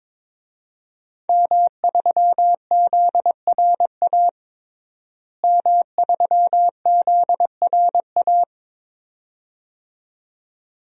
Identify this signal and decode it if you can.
Morse code, heard as a keyed tone, about 22 wpm, 700 Hz. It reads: M3ZRA M3ZRA
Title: DXCC-100 Call Signs - 5 Characters - Repeated 2x - 22wpm